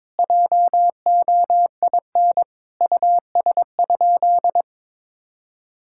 Morse code, heard as a keyed tone, about 22 wpm, 700 Hz. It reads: JOIN US?
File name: practice / 22wpm / Sentences with 4-Word Phrases - 22wpm